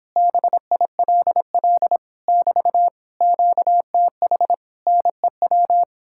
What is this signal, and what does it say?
Morse code, heard as a keyed tone, about 31 wpm, 700 Hz. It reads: BILL <BT> QTH NEW